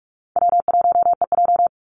Morse code, heard as a keyed tone, about 44 wpm, 700 Hz. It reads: W1EJ
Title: NA SOTA Activator Callsigns 45wpm